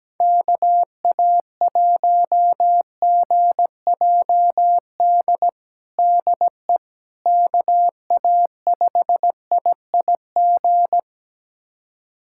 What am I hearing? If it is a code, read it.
Morse code, heard as a keyed tone, about 17 wpm, 700 Hz. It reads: KA1GJD DE KA5IIG
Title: USA Callsign DE Callsign - 6 Characters - 17wpm